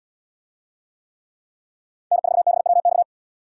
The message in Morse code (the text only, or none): N4BWX